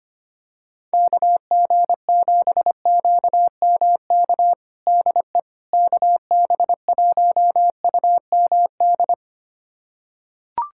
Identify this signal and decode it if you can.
Morse code, heard as a keyed tone, about 25 wpm, 700 Hz. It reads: KG7QMK DE KB1UMD
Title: USA Callsign DE Callsign - 6 Characters - 25wpm